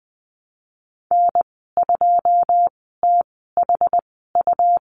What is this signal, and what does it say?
Morse code, heard as a keyed tone, about 20 wpm, spaced as for 15 wpm, 700 Hz. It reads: N2THU
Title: 128 Lesson 21g - Number 6 - Call Signs 20wpm